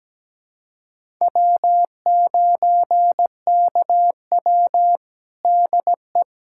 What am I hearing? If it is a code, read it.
Morse code, heard as a keyed tone, about 17 wpm, 700 Hz. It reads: W9KW DE
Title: USA Callsign DE Callsign 17wpm